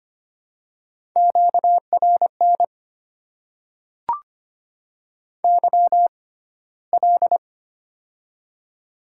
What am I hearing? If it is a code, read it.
Morse code, heard as a keyed tone, about 25 wpm, spaced as for 15 wpm, 700 Hz. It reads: QRN YL
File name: Top 90 QSO Elements - 6x Character Spacing - 25wpm